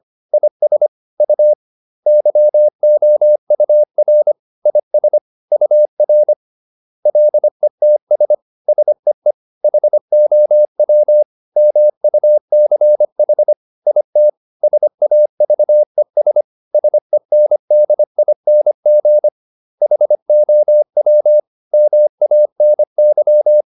Morse code, heard as a keyed tone, about 25 wpm, 600 Hz. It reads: IS U YOUR IS UR LETS SEE HOW MUCH IT SAVES SENDING HOW MANY
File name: SS319 25